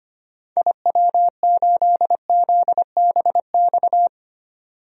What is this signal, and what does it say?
Morse code, heard as a keyed tone, about 25 wpm, 700 Hz. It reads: IW8ZBX
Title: DXCC-100 Call Signs - 6 Characters - 25wpm